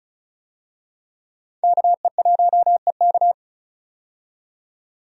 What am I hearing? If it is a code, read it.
Morse code, heard as a keyed tone, about 36 wpm, 700 Hz. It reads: KE1EK
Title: USA Call Signs - 5 Characters - 35wpm